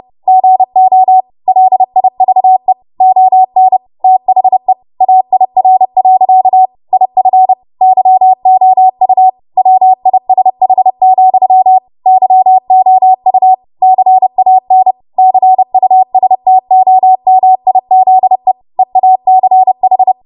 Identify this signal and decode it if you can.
Morse code, heard as a keyed tone, about 30 wpm, 750 Hz. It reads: GO LIVE ON THE AIR. IF YOU WISH, YOU CAN CUSTOMIZE EACH
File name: Morse / 30WPM